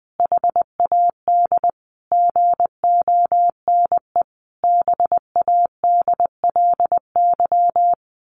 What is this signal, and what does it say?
Morse code, heard as a keyed tone, about 20 wpm, 700 Hz. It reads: HAD GONE BADLY
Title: A Sound Of Thunder 20wpm